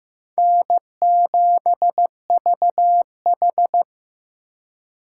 Morse code, heard as a keyed tone, about 15 wpm, 700 Hz. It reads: N7VH
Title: USA Call Signs - 3x Character Spacing - 15wpm